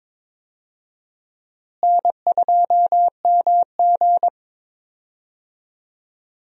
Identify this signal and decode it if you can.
Morse code, heard as a keyed tone, about 21 wpm, 700 Hz. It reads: N2MG
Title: USA Call Signs - 4 Characters - Repeated 3x - 22wpm